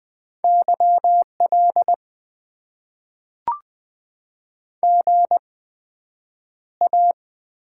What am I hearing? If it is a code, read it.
Morse code, heard as a keyed tone, about 20 wpm, spaced as for 9 wpm, 700 Hz. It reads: YL GA
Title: Top 90 QSO Elements - 8x Character Spacing - 20wpm